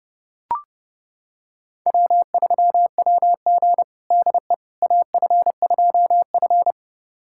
Morse code, heard as a keyed tone, about 32 wpm, 700 Hz. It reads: W3WG DE AF2F
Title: USA Callsign DE Callsign - 4 Characters - 30wpm